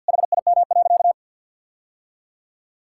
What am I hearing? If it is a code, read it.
Morse code, heard as a keyed tone, about 50 wpm, 700 Hz. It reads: HIM.
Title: Sentences from Top 1600 Words 50wpm